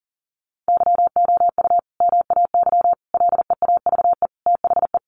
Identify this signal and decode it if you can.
Morse code, heard as a keyed tone, about 40 wpm, 700 Hz. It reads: YOU MAY LEAVE THE
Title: Sentences from Top 400 Words 40wpm